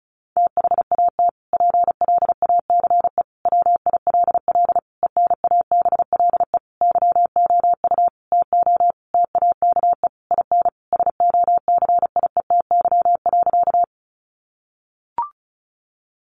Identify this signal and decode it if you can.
Morse code, heard as a keyed tone, about 34 wpm, 700 Hz. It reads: THAT PLACE WILL ENABLE YOU TO TAKE IN SOCIETY.
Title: Sentences from Top 1500 Words 35wpm